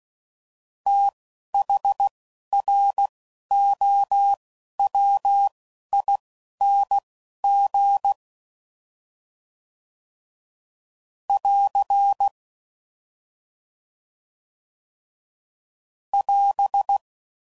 Morse code, heard as a keyed tone, about 16 wpm, spaced as for 12 wpm, 800 Hz. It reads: THROWING <AR> <AS>
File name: Prefixes / FLs & THs / TH Prefixes 1